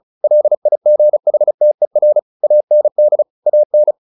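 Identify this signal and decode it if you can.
Morse code, heard as a keyed tone, about 34 wpm, 600 Hz. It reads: LIGHTER AND AN
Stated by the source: Morse practice at 35 wpm from CW med Gnister, sending Ken Follett's novel Never in English.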